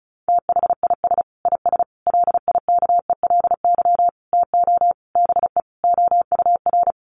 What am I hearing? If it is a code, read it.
Morse code, heard as a keyed tone, about 34 wpm, 700 Hz. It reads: THIS IS LIKELY TO BE OUR
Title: Sentences from Top 1400 Words 35wpm